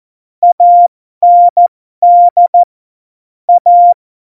Morse code, heard as a keyed tone, about 14 wpm, spaced as for 12 wpm, 700 Hz. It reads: AND A
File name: Morsekurs / Witze